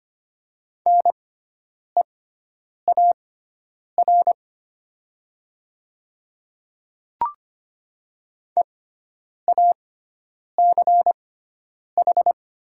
Morse code, heard as a keyed tone, about 25 wpm, spaced as for 9 wpm, 700 Hz. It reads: NEAR EACH
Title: 6x Character Spacing - Top 200 Words - 25wpm